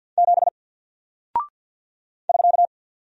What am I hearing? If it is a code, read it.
Morse code, heard as a keyed tone, about 48 wpm, 700 Hz. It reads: 7 <SK>
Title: Single Character (Letter:Number:Punctuation:Prosign) 50wpm